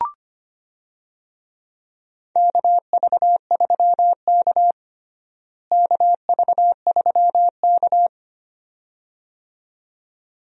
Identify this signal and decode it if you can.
Morse code, heard as a keyed tone, about 25 wpm, 700 Hz. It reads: KV3K KV3K
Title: USA Call Signs - 4 Characters - Repeated 2x - 25wpm